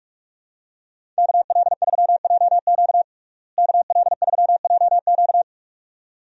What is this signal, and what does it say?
Morse code, heard as a keyed tone, about 46 wpm, 700 Hz. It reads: KL3JQ KL3JQ
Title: USA Call Signs - 5 Characters - Repeated 2x - 45wpm